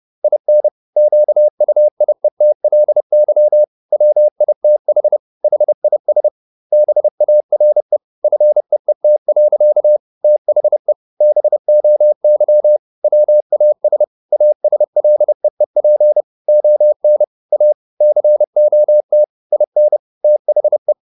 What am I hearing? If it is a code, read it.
Morse code, heard as a keyed tone, about 30 wpm, 600 Hz. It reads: IN QUIETLY WITH HIS BARE FEET. THE BOY WAS ASLEEP ON A COT IN THE
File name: the old man and the sea / the old man and the sea 1 30wpm